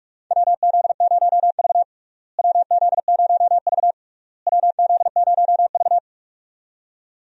Morse code, heard as a keyed tone, about 46 wpm, 700 Hz. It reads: WZ0V WZ0V WZ0V
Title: USA Call Signs - 4 Characters - Repeated 3x - 45wpm